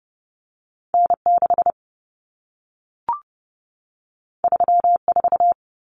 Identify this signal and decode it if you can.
Morse code, heard as a keyed tone, about 30 wpm, 700 Hz. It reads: N6 34
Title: Common Cut Numbers - 2-Digit 30wpm